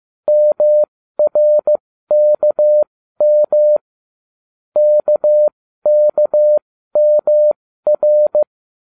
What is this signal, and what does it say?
Morse code, heard as a keyed tone, about 15 wpm, spaced as for 13 wpm, 600 Hz. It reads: MRKM KKMR